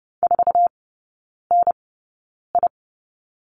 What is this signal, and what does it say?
Morse code, heard as a keyed tone, about 30 wpm, spaced as for 9 wpm, 700 Hz. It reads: 4NI
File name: Lesson 18e - Letter m - Three Characters 30wpm